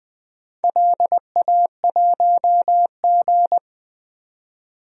Morse code, heard as a keyed tone, about 20 wpm, 700 Hz. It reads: LA1G
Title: DXCC-100 Call Signs - 4 Characters - 20wpm